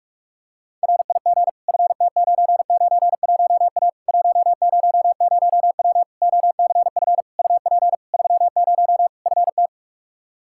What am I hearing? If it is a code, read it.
Morse code, heard as a keyed tone, about 42 wpm, 700 Hz. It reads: RIG FT991A 100W OCF UP 30 FT
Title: Practice QSO - Exchange 2 - Info - 45wpm